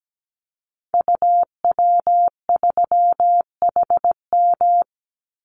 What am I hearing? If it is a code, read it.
Morse code, heard as a keyed tone, about 17 wpm, 700 Hz. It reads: UW3HM
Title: Call Signs Repeated 3x - 17wpm